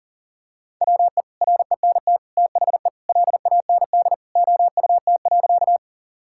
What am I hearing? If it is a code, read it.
Morse code, heard as a keyed tone, about 37 wpm, 700 Hz. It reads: WE RENT THE LAND OUT.
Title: Sentences from Top 1000 Words - Review Part 5 - 40wpm